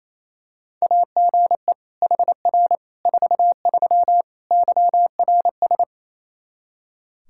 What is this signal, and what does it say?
Morse code, heard as a keyed tone, about 28 wpm, 700 Hz. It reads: AGE HR 43 YRS
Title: K1USN SST QSO Practice 28wpm